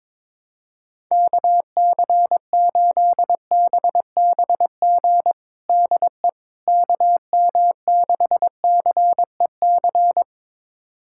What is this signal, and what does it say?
Morse code, heard as a keyed tone, about 22 wpm, 700 Hz. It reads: KC8BBG DE KM6CEC
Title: USA Callsign DE Callsign 22wpm